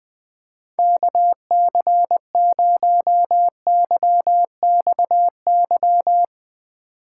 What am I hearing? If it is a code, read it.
Morse code, heard as a keyed tone, about 20 wpm, 700 Hz. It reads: KC0YXY
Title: Speed Racing - USA Call Signs 17wpm